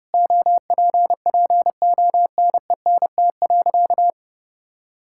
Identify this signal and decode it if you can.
Morse code, heard as a keyed tone, about 30 wpm, 700 Hz. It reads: OPPONENT.
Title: Pangram Challenge v2 - 30wpm